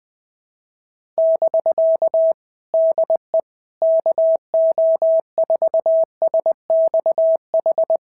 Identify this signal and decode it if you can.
Morse code, heard as a keyed tone, about 20 wpm, 650 Hz. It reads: <BK> DE KO4SXH
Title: POTA QSO Practice - 20wpm